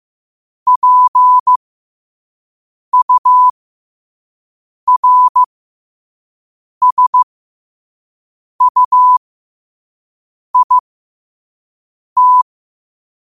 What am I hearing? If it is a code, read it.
Morse code, heard as a keyed tone, about 15 wpm, spaced as for 5 wpm, 1000 Hz. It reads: PURSUIT
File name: qotdfiles / 5wpm